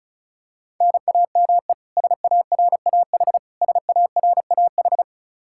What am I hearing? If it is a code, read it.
Morse code, heard as a keyed tone, about 34 wpm, 700 Hz. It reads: NAME SARAH SARAH
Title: Practice QSO - Exchange 1 - Beginning - 35wpm